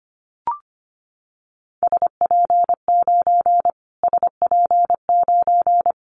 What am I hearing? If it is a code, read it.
Morse code, heard as a keyed tone, about 25 wpm, 700 Hz. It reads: SP9 SP9
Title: Callsign Prefixes for Top 25 DXCC - Repeated 3x - 25wpm